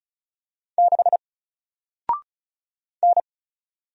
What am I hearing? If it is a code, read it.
Morse code, heard as a keyed tone, about 36 wpm, 700 Hz. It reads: 6 N
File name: Single-Letter-Number-35wpm